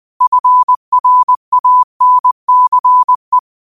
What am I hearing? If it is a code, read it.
Morse code, heard as a keyed tone, about 20 wpm, 1000 Hz. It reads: FRANCE
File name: qotdfiles / 20wpm